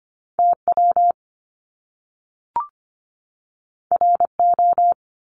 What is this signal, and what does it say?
Morse code, heard as a keyed tone, about 25 wpm, 700 Hz. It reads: TW RO
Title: Top Two Letter Combinations in Words 25wpm